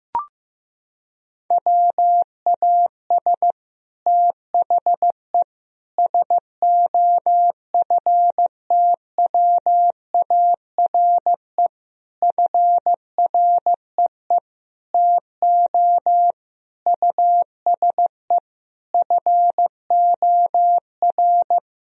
Morse code, heard as a keyed tone, about 15 wpm, 700 Hz. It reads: WAS THE SOFTWARE FREE TO USE FOR
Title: Sentences from Top 1000 Words - Review Part 3 - 15wpm